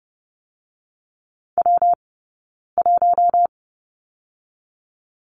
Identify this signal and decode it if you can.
Morse code, heard as a keyed tone, about 30 wpm, spaced as for 9 wpm, 700 Hz. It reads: W1
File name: Lesson 20d - Number 3 - Two Characters 30wpm